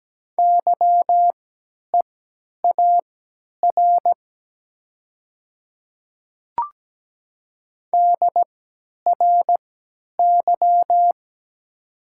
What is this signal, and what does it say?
Morse code, heard as a keyed tone, about 17 wpm, spaced as for 10 wpm, 700 Hz. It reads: YEAR DRY
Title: 3x Character Spacing - Top 500 Words - 17wpm